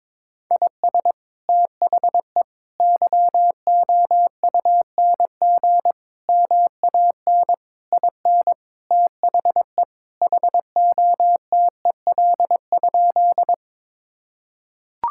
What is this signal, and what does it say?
Morse code, heard as a keyed tone, about 22 wpm, 700 Hz. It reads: IS THE YOUNG MAN IN THE HOTEL?
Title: Sentences from Top 1600 Words 22wpm